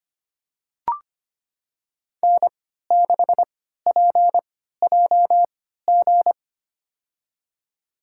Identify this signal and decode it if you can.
Morse code, heard as a keyed tone, about 25 wpm, spaced as for 14 wpm, 700 Hz. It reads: N6PJG
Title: USA Call Signs - 5 Characters - 3x Character Spacing - 25wpm